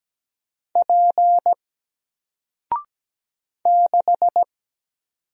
Morse code, heard as a keyed tone, about 17 wpm, 700 Hz. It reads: P 6
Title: Single Character (Letter:Number:Punctuation:Prosign) Alt 17wpm